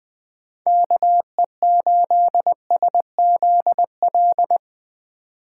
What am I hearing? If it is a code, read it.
Morse code, heard as a keyed tone, about 20 wpm, 700 Hz. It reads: KE8SZL
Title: 2023 POTA Activator Call Signs - 20wpm